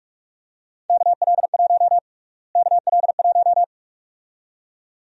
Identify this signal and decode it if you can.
Morse code, heard as a keyed tone, about 46 wpm, 700 Hz. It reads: KL1 KL1
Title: USA Call Sign Prefixes - Repeated 2x - 45wpm